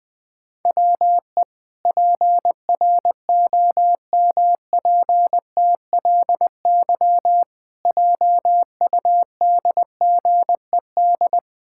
Morse code, heard as a keyed tone, about 20 wpm, 700 Hz. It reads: WE PROMPTLY JUDGED
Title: Pangram Challenge v2 - 20wpm